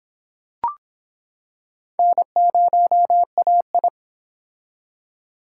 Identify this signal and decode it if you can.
Morse code, heard as a keyed tone, about 26 wpm, 700 Hz. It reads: N0AI